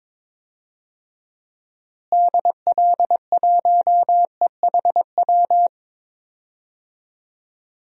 Morse code, heard as a keyed tone, about 22 wpm, 700 Hz. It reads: DL1EHW
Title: DXCC-100 Call Signs - 6 Characters - Repeated 2x - 22wpm